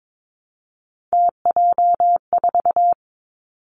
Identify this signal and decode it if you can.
Morse code, heard as a keyed tone, about 22 wpm, 700 Hz. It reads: TJ4
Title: Three Challenging Letter-Number Combinations 22wpm